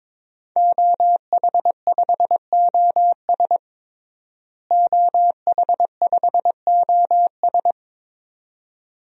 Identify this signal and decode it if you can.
Morse code, heard as a keyed tone, about 22 wpm, 700 Hz. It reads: OH5OS OH5OS
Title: DXCC-100 Call Signs - 5 Characters - Repeated 2x - 22wpm